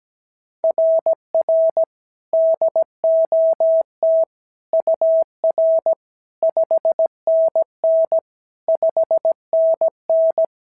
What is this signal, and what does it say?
Morse code, heard as a keyed tone, about 16 wpm, 650 Hz. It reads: RR DOT UR 5NN 5NN
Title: SOTA QSO Practice 17wpm